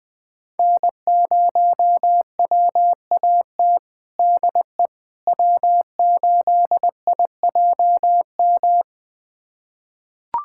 Morse code, heard as a keyed tone, about 21 wpm, 700 Hz. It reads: N0WAT DE W8IJM
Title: USA Callsign DE Callsign - 5 Characters - 20wpm